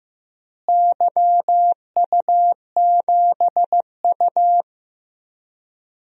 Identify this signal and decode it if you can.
Morse code, heard as a keyed tone, about 15 wpm, 700 Hz. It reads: YU7U
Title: DXCC-100 Call Signs - 4 Characters - Repeated 2x - 15wpm